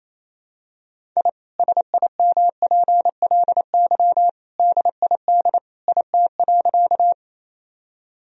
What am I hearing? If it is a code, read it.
Morse code, heard as a keyed tone, about 28 wpm, 700 Hz. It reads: I SIMPLY DID IT.
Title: Sentences from Top 1000 Words - Review Part 4 - 28wpm